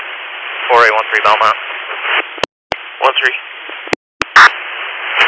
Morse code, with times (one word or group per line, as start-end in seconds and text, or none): none